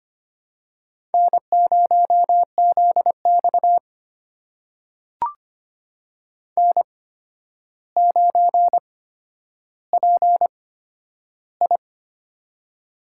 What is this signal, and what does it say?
Morse code, heard as a keyed tone, about 25 wpm, spaced as for 11 wpm, 700 Hz. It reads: N0ZX N9PI
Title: USA Call Signs - 4 Characters - 8x Character Spacing - 25wpm